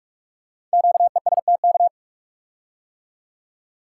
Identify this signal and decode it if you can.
Morse code, heard as a keyed tone, about 44 wpm, 700 Hz. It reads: QESTK
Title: Code Groups - 5 Letters - 45wpm